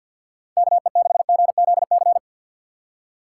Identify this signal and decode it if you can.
Morse code, heard as a keyed tone, about 47 wpm, 700 Hz. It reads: KE6GZC
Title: USA Call Signs - 6 Characters - 6x Character Spacing - 50wpm